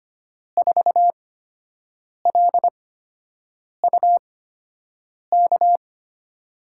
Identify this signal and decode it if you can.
Morse code, heard as a keyed tone, about 25 wpm, spaced as for 7 wpm, 700 Hz. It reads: 4LUK
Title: USA Call Signs - 8x Character Spacing - 25wpm